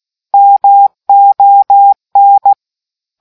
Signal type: Morse code